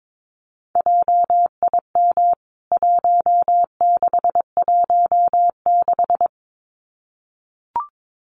Morse code, heard as a keyed tone, about 22 wpm, 700 Hz. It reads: JIM 1616
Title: CWT Exchange 22wpm